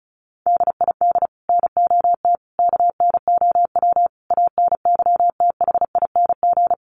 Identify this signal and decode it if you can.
Morse code, heard as a keyed tone, about 34 wpm, 700 Hz. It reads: DID NOT KNOW ANYTHING
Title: Sets of 5 Words Encore2 35wpm